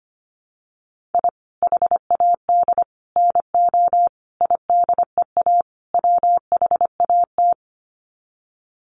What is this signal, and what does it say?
Morse code, heard as a keyed tone, about 25 wpm, 700 Hz. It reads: I HAD NO IDEA WHAT
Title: Sets of 5 Words Encore 25wpm